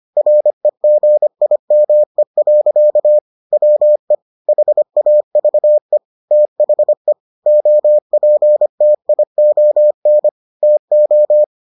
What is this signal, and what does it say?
Morse code, heard as a keyed tone, about 25 wpm, 600 Hz. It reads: REGIME. WE HAVE THE OPTION TO